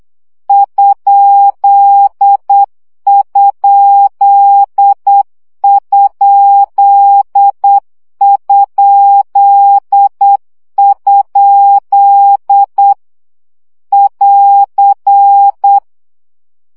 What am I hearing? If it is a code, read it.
Morse code, heard as a keyed tone, about 8 wpm, 800 Hz. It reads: ????? <AR>
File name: CW gyakorlas kerdojel 7 baud